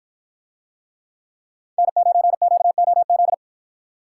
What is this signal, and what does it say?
Morse code, heard as a keyed tone, about 53 wpm, 700 Hz. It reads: N9QOZ